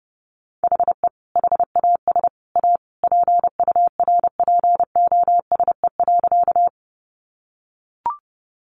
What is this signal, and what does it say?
Morse code, heard as a keyed tone, about 30 wpm, 700 Hz. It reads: HE HAS A PURPOSE.
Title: Sentences from Top 800 Words 30wpm